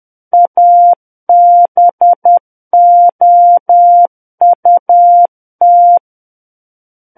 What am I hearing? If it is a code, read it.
Morse code, heard as a keyed tone, about 10 wpm, 700 Hz. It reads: ABOUT
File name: Book One / 01-Chapter 1 - 10 WPM